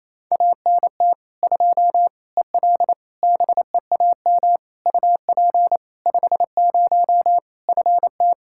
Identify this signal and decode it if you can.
Morse code, heard as a keyed tone, about 27 wpm, 700 Hz. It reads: ANT 2 EL BEAM UP 50 FT
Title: Practice QSO - Exchange 2 - Info - 28wpm